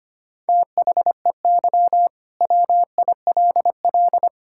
Morse code, heard as a keyed tone, about 25 wpm, 700 Hz. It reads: THEY WILL
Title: Sentences from Top 1000 Words - Review Part 5 - 25wpm